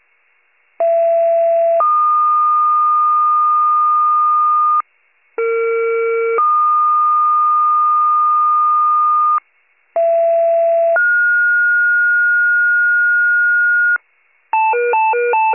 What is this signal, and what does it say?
Two-tone page, heard as a one-second tone, then a three-second tone, about 670 then 1220 Hz, about 460 then 1220 Hz, about 670 then 1430 Hz. It is heard from 0.8 to 14.0 s.
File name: pager / 78 FIRE / wcalert